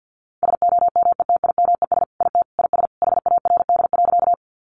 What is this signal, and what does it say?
Morse code, heard as a keyed tone, about 48 wpm, 700 Hz. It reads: SOMETIMES IT IS HARD.
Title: Sentences from Top 600 Words 50wpm